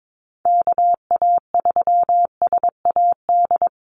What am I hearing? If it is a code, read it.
Morse code, heard as a keyed tone, about 22 wpm, 700 Hz. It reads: KA3SAD
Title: USA Call Signs Repeated 2x 22wpm